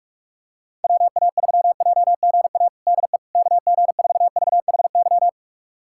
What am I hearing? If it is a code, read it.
Morse code, heard as a keyed tone, about 44 wpm, 700 Hz. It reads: WA3JGA DE KG4VHY
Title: USA Callsign DE Callsign 45wpm